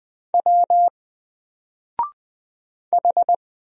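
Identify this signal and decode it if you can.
Morse code, heard as a keyed tone, about 20 wpm, 700 Hz. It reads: W H
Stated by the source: Single letter and number Morse drill at 20 wpm, from N4LCD.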